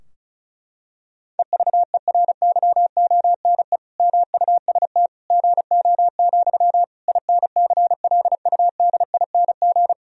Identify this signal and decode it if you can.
Morse code, heard as a keyed tone, about 34 wpm, 700 Hz. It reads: EVERYONE MUST GO, INCLUDING
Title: Sentences from Top 1000 Words - Review Part 4 - 35wpm